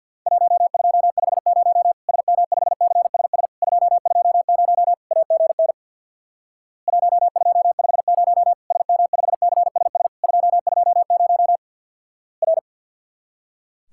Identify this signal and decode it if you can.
Morse code, heard as a keyed tone, about 50 wpm, 700 Hz. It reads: 1250 SM5CSS 220 AGN 1250 SM5CSS 220 R
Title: Worked All Europe DX Contest - QTC Exchange with Repeats - 50wpm